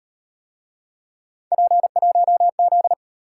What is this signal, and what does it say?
Morse code, heard as a keyed tone, about 34 wpm, 700 Hz. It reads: P1Z